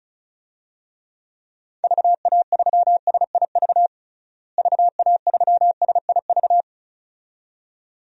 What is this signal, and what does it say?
Morse code, heard as a keyed tone, about 34 wpm, 700 Hz. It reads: VA3SIV VA3SIV
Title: DXCC-100 Call Signs - 6 Characters - Repeated 2x - 35wpm